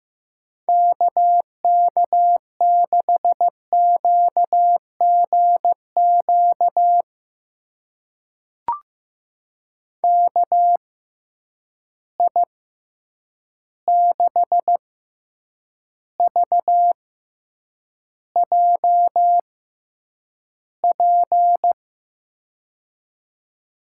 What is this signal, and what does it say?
Morse code, heard as a keyed tone, about 15 wpm, spaced as for 8 wpm, 700 Hz. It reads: KK6QGQ KI6VJP
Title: USA Call Signs - 6 Characters - 6x Character Spacing - 15wpm